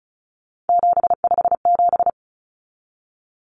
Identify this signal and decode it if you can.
Morse code, heard as a keyed tone, about 34 wpm, 700 Hz. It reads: 757
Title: Common Cut Numbers - 3-Digit 35wpm